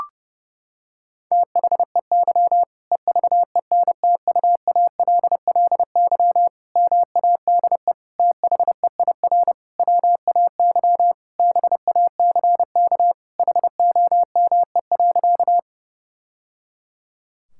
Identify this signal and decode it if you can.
Morse code, heard as a keyed tone, about 32 wpm, 700 Hz. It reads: THEY EVENTUALLY MADE THEIR WAY BACK HOME.
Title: Sentences from Top 1000 Words - Review Part 6 - 30wpm